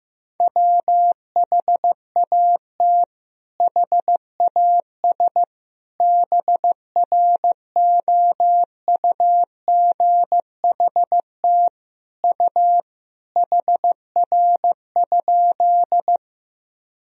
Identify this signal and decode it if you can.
Morse code, heard as a keyed tone, about 16 wpm, 700 Hz. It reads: WHAT HAS BROUGHT U HR?
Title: Sentences from Top 500 Words and Abbreviations 15wpm